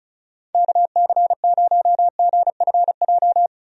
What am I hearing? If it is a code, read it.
Morse code, heard as a keyed tone, about 36 wpm, 700 Hz. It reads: KC0GFJ
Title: USA Call Signs - 6 Characters - 35wpm